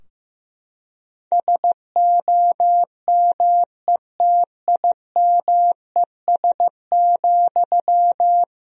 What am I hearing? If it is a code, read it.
Morse code, heard as a keyed tone, about 15 wpm, 700 Hz. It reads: SOMETIMES,
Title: Sentences from Top 1000 Words - Review Part 3 - 15wpm